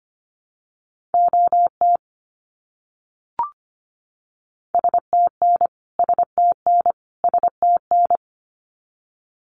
Morse code, heard as a keyed tone, about 25 wpm, 700 Hz. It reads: OT STN STN STN
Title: Abbreviations Repeated 3x - 25wpm